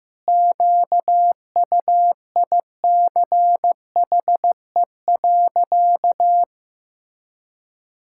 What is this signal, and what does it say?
Morse code, heard as a keyed tone, about 15 wpm, 700 Hz. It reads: QUICHE.
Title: Pangram Challenge v2 - 15wpm